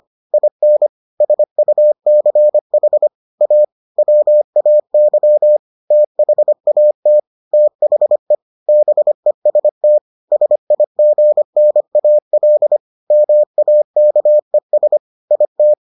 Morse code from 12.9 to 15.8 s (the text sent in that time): MAKES IT